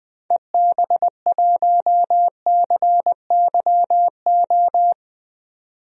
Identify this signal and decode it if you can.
Morse code, heard as a keyed tone, about 20 wpm, 700 Hz. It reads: EB1CYO
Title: DXCC-100 Call Signs - 6 Characters - 20wpm